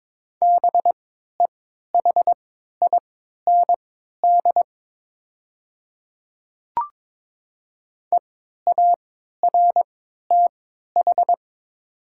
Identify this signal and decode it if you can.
Morse code, heard as a keyed tone, about 23 wpm, spaced as for 13 wpm, 700 Hz. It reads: BEHIND EARTH
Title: 3x Character Spacing - Top 500 Words - 22wpm